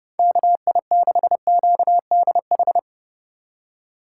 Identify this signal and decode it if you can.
Morse code, heard as a keyed tone, about 30 wpm, 700 Hz. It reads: KI6QDH
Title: K1USN SST QSO Practice - 30wpm